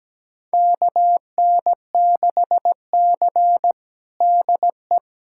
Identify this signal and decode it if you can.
Morse code, heard as a keyed tone, about 17 wpm, 700 Hz. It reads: KN6C DE
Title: USA Callsign DE Callsign - 4 Characters - 17wpm